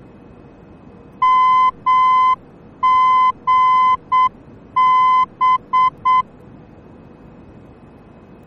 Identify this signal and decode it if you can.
Morse code, heard as a keyed tone, about 7 wpm, 1020 Hz. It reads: MGB
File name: germany / mx / MGB-D